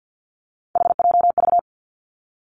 Morse code, heard as a keyed tone, about 50 wpm, 700 Hz. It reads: HJV
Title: Three Challenging Letter-Number Combinations 50wpm